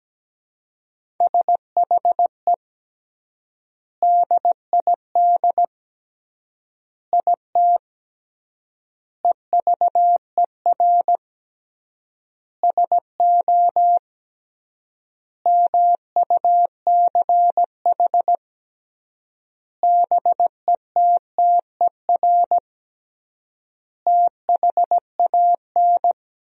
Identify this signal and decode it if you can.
Morse code, heard as a keyed tone, about 17 wpm, 700 Hz. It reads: SHE DID IT EVER SO MUCH BETTER THAN
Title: 3x Word-Spacing - Sentences from Top 400 Words - 17wpm